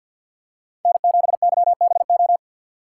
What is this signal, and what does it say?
Morse code, heard as a keyed tone, about 52 wpm, 700 Hz. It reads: N7YBO
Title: USA Call Signs - 5 Characters - 2x Character Spacing - 50wpm